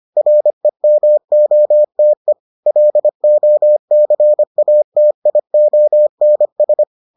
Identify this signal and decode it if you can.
Morse code, heard as a keyed tone, about 25 wpm, 600 Hz. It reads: REMOTE LOCATIONS